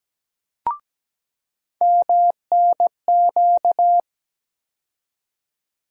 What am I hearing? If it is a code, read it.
Morse code, heard as a keyed tone, about 17 wpm, 700 Hz. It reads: MNQ
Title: USA Call Sign Postfixes - 3 Letters 17wpm